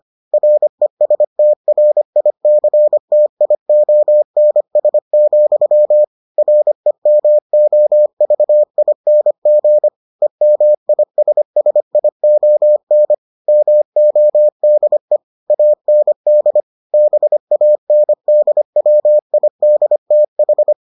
Morse code, heard as a keyed tone, about 25 wpm, 600 Hz. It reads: RESTRICTIONS, REMOVING EMISSION MODE AND BANDWIDTH